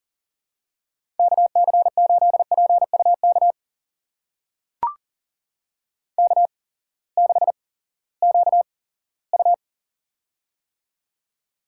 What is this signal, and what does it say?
Morse code, heard as a keyed tone, about 37 wpm, spaced as for 20 wpm, 700 Hz. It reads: KC8PUK K6QU
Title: USA Call Signs - 8x Character Spacing - 40wpm